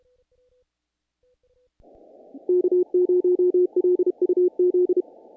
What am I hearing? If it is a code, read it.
Morse code, heard as a keyed tone, about 32 wpm, 360 Hz. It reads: K0LUZ
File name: contests / naqp-cw-k5tr-op-k5pi- -1800